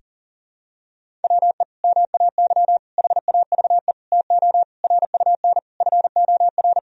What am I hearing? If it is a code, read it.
Morse code, heard as a keyed tone, about 40 wpm, 700 Hz. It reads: WE MAY HAVE TO RUN FOR
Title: Sentences from Top 1000 Words - Review Part 1 - 40wpm